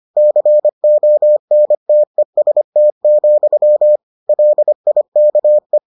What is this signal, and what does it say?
Morse code, heard as a keyed tone, about 25 wpm, 600 Hz. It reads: CONTEST, LIKE